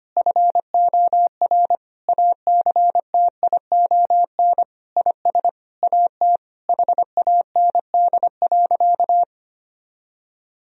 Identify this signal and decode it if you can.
Morse code, heard as a keyed tone, about 25 wpm, 700 Hz. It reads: FOR ACTION IS AT HAND.
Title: Sentences from Top 1000 Words - Review Part 4 - 25wpm